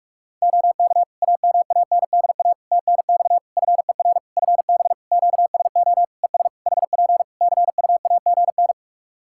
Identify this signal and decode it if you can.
Morse code, heard as a keyed tone, about 41 wpm, 700 Hz. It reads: OK AMANDA TNX FER FB QSO ES HP CUAGN
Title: K1USN SST QSO Practice 45wpm